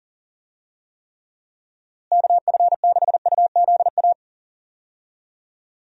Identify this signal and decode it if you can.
Morse code, heard as a keyed tone, about 40 wpm, 700 Hz. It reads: KF6UZA
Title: USA Call Signs - 6 Characters - Repeated 3x - 40wpm